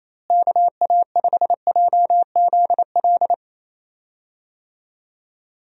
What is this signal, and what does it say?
Morse code, heard as a keyed tone, about 28 wpm, 700 Hz. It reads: KA5JZL
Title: USA Call Signs - 6 Characters - Repeated 2x - 28wpm